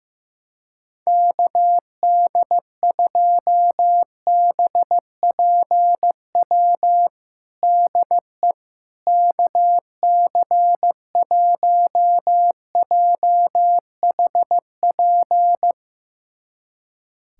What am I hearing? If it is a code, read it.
Morse code, heard as a keyed tone, about 16 wpm, 700 Hz. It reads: KD2BPW DE KC1JHP
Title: USA Callsign DE Callsign 15wpm